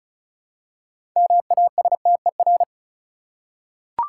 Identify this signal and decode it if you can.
Morse code, heard as a keyed tone, about 34 wpm, 700 Hz. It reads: MASTER
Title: Six Letter Words - 35wpm